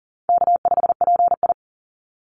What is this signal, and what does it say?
Morse code, heard as a keyed tone, about 40 wpm, 700 Hz. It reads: K5PI
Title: Easy Call Signs Repeated 3x - 40wpm